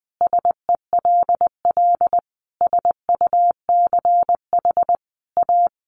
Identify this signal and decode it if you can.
Morse code, heard as a keyed tone, about 20 wpm, 700 Hz. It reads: SELL SUCH A
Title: Sentences from Top 1100 Words 20wpm